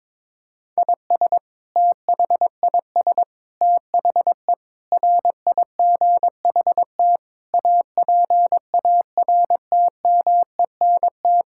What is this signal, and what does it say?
Morse code, heard as a keyed tone, about 22 wpm, 700 Hz. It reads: IS THIS THE RIGHT APARTMENT
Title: Sentences from Top 1000 Words - Review Part 6 - 22wpm